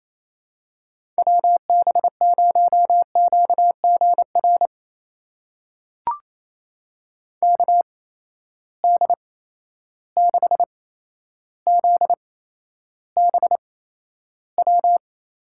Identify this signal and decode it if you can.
Morse code, heard as a keyed tone, about 28 wpm, spaced as for 12 wpm, 700 Hz. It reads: WB0QGR KD6ZBW
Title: USA Call Signs - 6 Characters - 8x Character Spacing - 28wpm